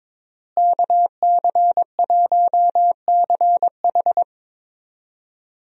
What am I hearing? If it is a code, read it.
Morse code, heard as a keyed tone, about 22 wpm, 700 Hz. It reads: KC1CH
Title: USA Call Signs - 5 Characters - Repeated 3x - 22wpm